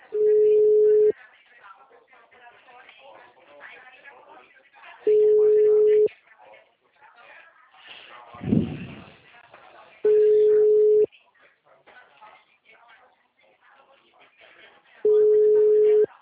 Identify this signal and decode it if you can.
Ringback tone (Europe, 425 Hz), with speech.
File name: out- -2004- - - .38082